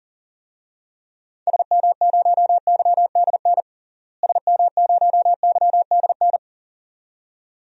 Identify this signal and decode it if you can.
Morse code, heard as a keyed tone, about 37 wpm, 700 Hz. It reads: SM0YDN SM0YDN
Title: DXCC-100 Call Signs - 6 Characters - Repeated 2x - 40wpm